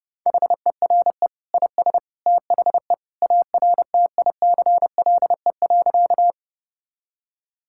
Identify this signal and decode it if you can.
Morse code, heard as a keyed tone, about 30 wpm, 700 Hz. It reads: HERE IS THE ARTICLE.
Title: Sentences from Top 1000 Words - Review Part 5 - 30wpm